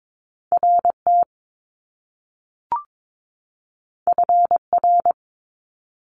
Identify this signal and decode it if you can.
Morse code, heard as a keyed tone, about 22 wpm, 700 Hz. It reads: RT FR
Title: Top Two Letter Combinations in Words 22wpm